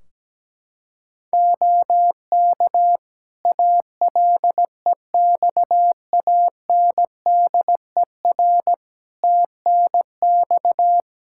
Morse code, heard as a keyed tone, about 17 wpm, 700 Hz. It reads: OK ALEXANDER TNX
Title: K1USN SST QSO Practice 17wpm